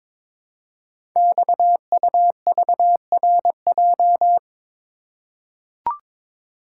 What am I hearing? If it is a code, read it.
Morse code, heard as a keyed tone, about 22 wpm, 700 Hz. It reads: XUVRJ
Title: Code Groups - 5 Letters - 22wpm